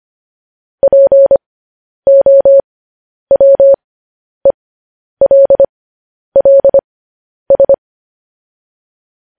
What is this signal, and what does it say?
Morse code, heard as a keyed tone, about 25 wpm, spaced as for 10 wpm, 550 Hz. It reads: POWELLS
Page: News Headlines in Morse Code at 10 WPM